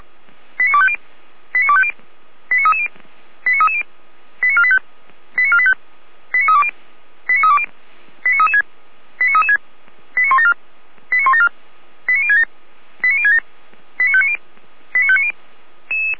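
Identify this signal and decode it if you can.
Five-tone selective call (ZVEI), heard as quick runs of five tones, tones of about 70 ms. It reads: 89250 89250 89330 89330 89485 89485 89320 89320 89336 89336 89174 89174 89067 89067 89590 89590
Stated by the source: ZVEI five-tone selective call 89290 received on an SDR.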